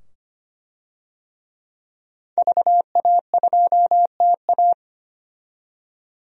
Morse code, heard as a keyed tone, about 25 wpm, 700 Hz. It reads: VA2TA
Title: DXCC-100 Call Signs - 5 Characters - 25wpm